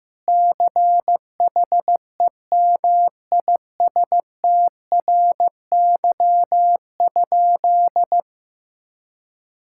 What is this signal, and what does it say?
Morse code, heard as a keyed tone, about 15 wpm, 700 Hz. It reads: CHEMISTRY?
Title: Sentences from Top 1000 Words - Review Part 6 - 15wpm